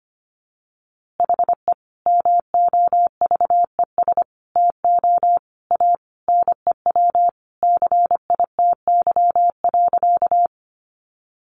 Morse code, heard as a keyed tone, about 25 wpm, 700 Hz. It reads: HE MOVES TO A NEW CITY.
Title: Easy Phrases 25wpm